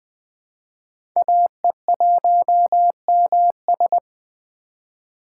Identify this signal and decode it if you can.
Morse code, heard as a keyed tone, about 20 wpm, 700 Hz. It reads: AE1MS
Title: 2023 SOTA Chaser Call Signs - 20wpm